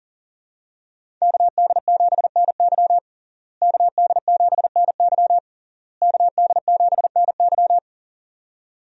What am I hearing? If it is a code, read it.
Morse code, heard as a keyed tone, about 40 wpm, 700 Hz. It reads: KD7NY KD7NY KD7NY
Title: USA Call Signs - 5 Characters - Repeated 3x - 40wpm